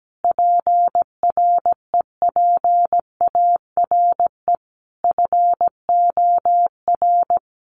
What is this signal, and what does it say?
Morse code, heard as a keyed tone, about 17 wpm, 700 Hz. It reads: PREPARE FOR
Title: Sentences from Top 1200 Words 17wpm